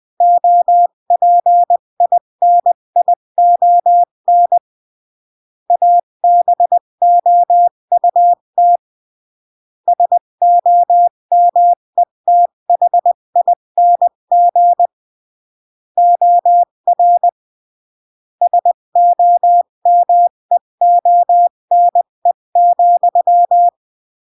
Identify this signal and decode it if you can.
Morse code, heard as a keyed tone, about 20 wpm, 700 Hz. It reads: OPINION ABOUT SOMETHING OR SOMEONE,
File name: CW / jodorowsky / 82 commandments 20-15